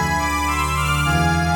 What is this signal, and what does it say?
There is no code.